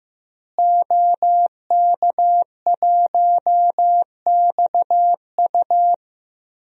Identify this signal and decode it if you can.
Morse code, heard as a keyed tone, about 15 wpm, 700 Hz. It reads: OK1XU
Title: DXCC-100 Call Signs - 5 Characters - Repeated 2x - 15wpm